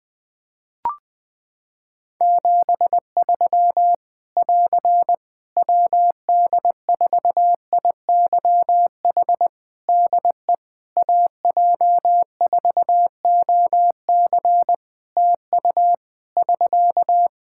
Morse code, heard as a keyed tone, about 20 wpm, 700 Hz. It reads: 73 <AR> WD4IYH DE AJ4OC TU <SK>
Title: Practice QSO - Exchange 4 - Ending - 20wpm